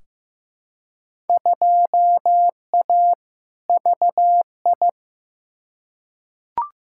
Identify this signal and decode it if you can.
Morse code, heard as a keyed tone, about 15 wpm, 700 Hz. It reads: 2A VI
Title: Field Day Exchanges Encore - 15wpm